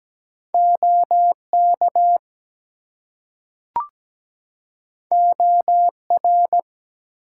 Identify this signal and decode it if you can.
Morse code, heard as a keyed tone, about 17 wpm, 700 Hz. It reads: OK OR
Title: Two Letter Words - 17wpm